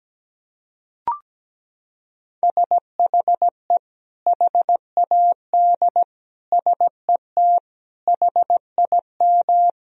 Morse code, heard as a keyed tone, about 17 wpm, 700 Hz. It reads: SHE HAD SET HIM
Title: Sentences from Top 1000 Words - Review Part 1 - 17wpm